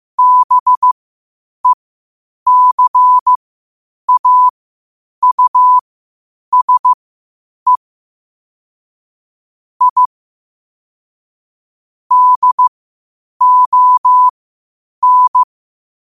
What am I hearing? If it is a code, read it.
Morse code, heard as a keyed tone, about 15 wpm, spaced as for 8 wpm, 1000 Hz. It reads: BECAUSE I DON
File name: qotdfiles / 8wpm